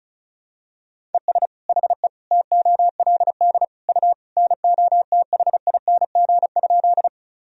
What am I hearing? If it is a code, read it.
Morse code, heard as a keyed tone, about 36 wpm, 700 Hz. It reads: ES HE TOLD U NOTHING?
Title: Sentences from Top 500 Words and Abbreviations 35wpm